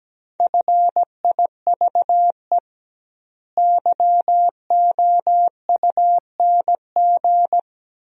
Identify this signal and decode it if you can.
Morse code, heard as a keyed tone, about 17 wpm, 700 Hz. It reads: FIVE YOUNG
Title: 2x Word-Spacing - Sentences from Top 400 Words - 17wpm